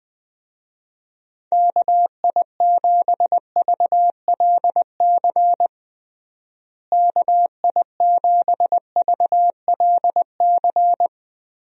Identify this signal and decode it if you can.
Morse code, heard as a keyed tone, about 20 wpm, 700 Hz. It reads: KI7VLC KI7VLC
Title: USA Call Signs - 6 Characters - Repeated 3x - 20wpm